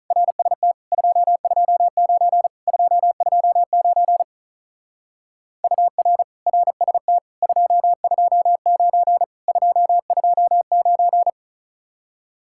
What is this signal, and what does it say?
Morse code, heard as a keyed tone, about 43 wpm, 700 Hz. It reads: RST 229 229 UR RST 229 229
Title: Speed-Racing Common QSO Phrases Encore - 35wpm